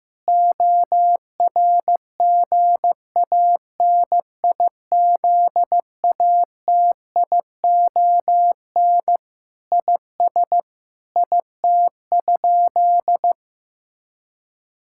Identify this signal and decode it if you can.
Morse code, heard as a keyed tone, about 15 wpm, 700 Hz. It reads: ORGANIZATION IS IT?
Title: Sentences from Top 1000 Words - Review Part 5 - 15wpm